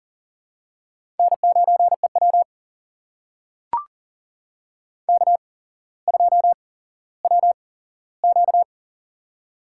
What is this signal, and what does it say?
Morse code, heard as a keyed tone, about 40 wpm, spaced as for 17 wpm, 700 Hz. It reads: N9EW K2WQ
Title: USA Call Signs - 4 Characters - 8x Character Spacing - 40wpm